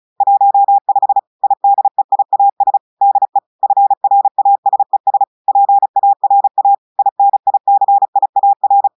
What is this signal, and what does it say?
Morse code, heard as a keyed tone, about 33 wpm, 800 Hz. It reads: 15 IDEIAS DE FRASES PARA INICIAR
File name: audios / portugais / portugais texte en clair 2 vitesse 35 mots minutes